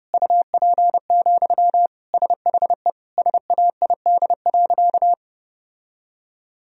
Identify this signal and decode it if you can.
Morse code, heard as a keyed tone, about 30 wpm, 700 Hz. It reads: UP, SHE SAID.
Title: Sentences from Top 1000 Words - Review Part 1 - 30wpm